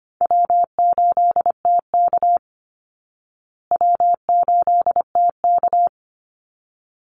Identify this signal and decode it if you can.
Morse code, heard as a keyed tone, about 25 wpm, 700 Hz. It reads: W8TK W8TK
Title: Easy Call Signs Repeated 2x - 25wpm